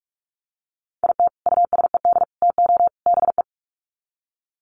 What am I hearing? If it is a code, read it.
Morse code, heard as a keyed tone, about 44 wpm, 700 Hz. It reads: IT USED TO BE
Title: Sets of 4 Words 45wpm